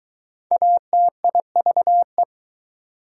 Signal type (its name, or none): Morse code